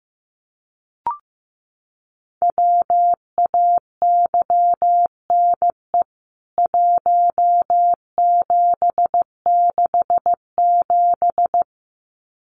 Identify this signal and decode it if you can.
Morse code, heard as a keyed tone, about 15 wpm, 700 Hz. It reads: WAYNE 1767
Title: CWT Exchange 15wpm